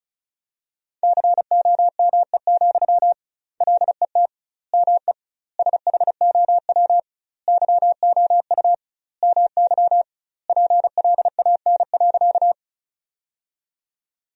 Morse code, heard as a keyed tone, about 34 wpm, 700 Hz. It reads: COME, LET ME SHOW YOU MY PLAN.
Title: 2x Word-Spacing - Sentences from Top 500 Words - 35wpm